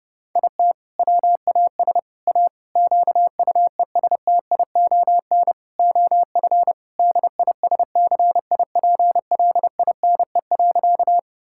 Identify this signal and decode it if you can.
Morse code, heard as a keyed tone, about 30 wpm, 700 Hz. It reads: IT WAS A QUESTION OF DISCIPLINE.
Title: Sentences from Top 1000 Words - Review Part 6 - 30wpm